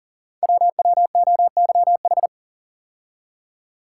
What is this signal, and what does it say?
Morse code, heard as a keyed tone, about 44 wpm, 700 Hz. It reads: WWOYH
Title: Code Groups - 5 Letters - 40wpm